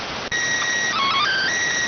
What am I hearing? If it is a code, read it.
SSTV (slow-scan television), PD 290.